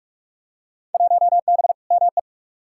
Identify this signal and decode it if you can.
Morse code, heard as a keyed tone, about 43 wpm, 700 Hz. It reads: 1B ME
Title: Field Day Exchanges - 45wpm